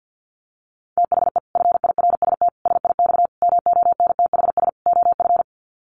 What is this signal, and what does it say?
Morse code, heard as a keyed tone, about 50 wpm, 700 Hz. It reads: THE FIRST SIX MONTHS OF
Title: Sets of 5 Words Encore2 50wpm